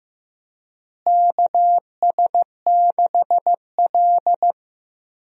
Morse code, heard as a keyed tone, about 15 wpm, 700 Hz. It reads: KS6L
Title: USA Call Signs - 4 Characters - Repeated 2x - 15wpm